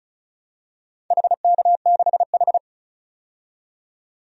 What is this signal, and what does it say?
Morse code, heard as a keyed tone, about 36 wpm, 700 Hz. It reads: HK6H
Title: DXCC-100 Call Signs - 4 Characters - 35wpm